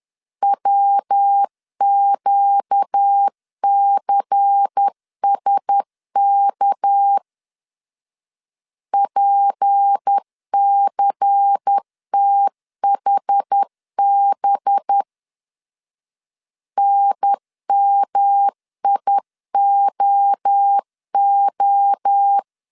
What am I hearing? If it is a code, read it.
Morse code, heard as a keyed tone, about 10 wpm, 800 Hz. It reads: WQCSK PCTHB NMIOO
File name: CW / DET / 1 6 1500C S B 10wpm 4137